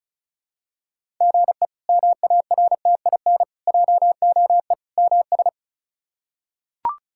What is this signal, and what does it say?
Morse code, heard as a keyed tone, about 34 wpm, 700 Hz. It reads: GE MARTIN JOE MS
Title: K1USN SST QSO Practice - 35wpm